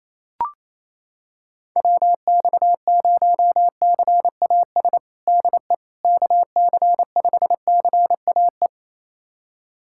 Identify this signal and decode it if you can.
Morse code, heard as a keyed tone, about 28 wpm, 700 Hz. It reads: WX0CAS DE KC5CAE
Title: USA Callsign DE Callsign - 6 Characters - 28wpm